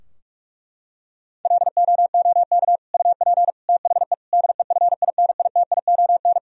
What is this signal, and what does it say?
Morse code, heard as a keyed tone, about 41 wpm, 700 Hz. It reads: LOOK UP THE DEFINITION
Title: Sentences from Top 1000 Words - Review Part 4 - 45wpm